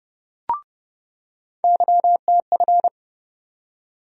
Morse code, heard as a keyed tone, about 30 wpm, 700 Hz. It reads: YTF
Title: USA Call Sign Postfixes - 3 Letters 30wpm